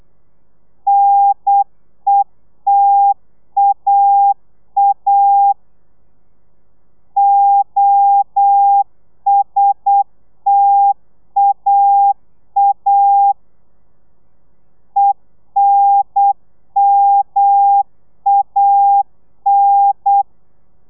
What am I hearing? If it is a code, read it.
Morse code, heard as a keyed tone, about 8 wpm, 800 Hz. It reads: NETAA OSTAA ENMAN